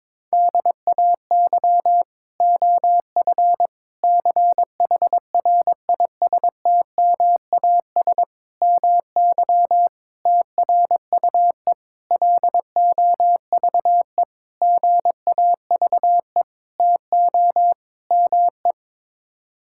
Morse code, heard as a keyed tone, about 22 wpm, 700 Hz. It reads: DAY OF CHRISTMAS MY TRUE LOVE GAVE TO ME
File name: Sets of 9 Words - 22wpm